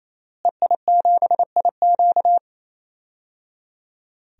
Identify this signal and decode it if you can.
Morse code, heard as a keyed tone, about 28 wpm, 700 Hz. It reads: EI7IQ
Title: DXCC-100 Call Signs - 5 Characters - Repeated 3x - 28wpm